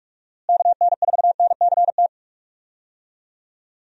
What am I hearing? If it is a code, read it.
Morse code, heard as a keyed tone, about 41 wpm, 700 Hz. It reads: KN4NCT
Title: USA Call Signs - 6 Characters - Repeated 2x - 45wpm